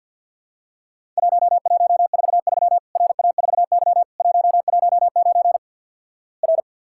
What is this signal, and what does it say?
Morse code, heard as a keyed tone, about 50 wpm, 700 Hz. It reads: 1143 RA4Y 119 R
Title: Worked All Europe DX Contest - QTC Exchange - 50wpm